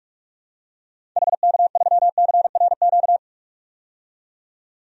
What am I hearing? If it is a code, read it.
Morse code, heard as a keyed tone, about 46 wpm, 700 Hz. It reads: HK3CRQ
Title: DXCC-100 Call Signs - 6 Characters - 45wpm